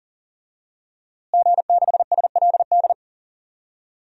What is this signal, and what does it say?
Morse code, heard as a keyed tone, about 37 wpm, 700 Hz. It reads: G6SLD
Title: DXCC-100 Call Signs - 5 Characters - Repeated 3x - 40wpm